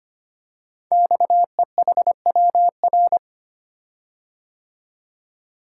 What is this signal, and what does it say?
Morse code, heard as a keyed tone, about 25 wpm, 700 Hz. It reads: XEHWR
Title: Code Groups - 5 Letters - 25wpm